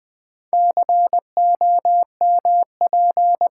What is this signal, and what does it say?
Morse code, heard as a keyed tone, about 20 wpm, 700 Hz. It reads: COMP
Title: Build A Word - 2 Letters At A Time - 8-Letter Words - 20wpm